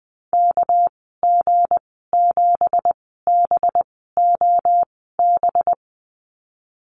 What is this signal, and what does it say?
Morse code, heard as a keyed tone, about 20 wpm, spaced as for 14 wpm, 700 Hz. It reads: KG7BOB
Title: 207 Lesson 33g - Letter j - Call Signs 20wpm